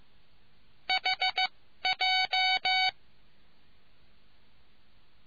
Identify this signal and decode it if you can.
Morse code, heard as a keyed tone, about 15 wpm, spaced as for 12 wpm, 750 Hz. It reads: HJ